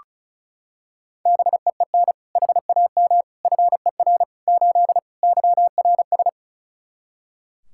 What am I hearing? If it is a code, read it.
Morse code, heard as a keyed tone, about 36 wpm, 700 Hz. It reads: BEEN HAM FER 8 YRS
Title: Practice QSO - Exchange 3 - More Info - 35wpm